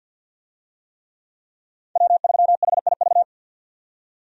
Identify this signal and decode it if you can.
Morse code, heard as a keyed tone, about 52 wpm, 700 Hz. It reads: W3HIV
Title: USA Call Signs - 5 Characters - 50wpm